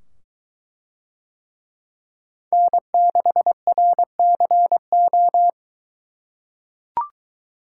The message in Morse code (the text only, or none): N6RCO